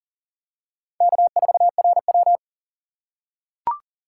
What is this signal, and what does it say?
Morse code, heard as a keyed tone, about 40 wpm, 700 Hz. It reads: K4RW
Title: USA Call Signs - 6x Character Spacing - 40wpm